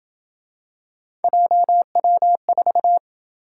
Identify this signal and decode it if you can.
Morse code, heard as a keyed tone, about 27 wpm, 700 Hz. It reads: JW4